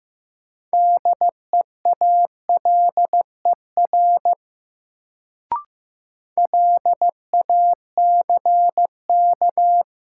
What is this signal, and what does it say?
Morse code, heard as a keyed tone, about 16 wpm, 700 Hz. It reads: DEALER LACK
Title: Sentences from Top 1100 Words - New Word Drill - 15wpm